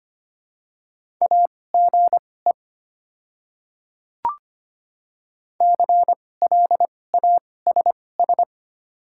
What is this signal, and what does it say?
Morse code, heard as a keyed tone, about 25 wpm, spaced as for 18 wpm, 700 Hz. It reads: AGE CLASS
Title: 2x Character Spacing - Top 500 Words - 25wpm